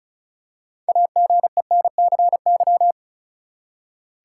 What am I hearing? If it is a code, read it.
Morse code, heard as a keyed tone, about 36 wpm, 700 Hz. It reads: AGENCY
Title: Six Letter Words - 35wpm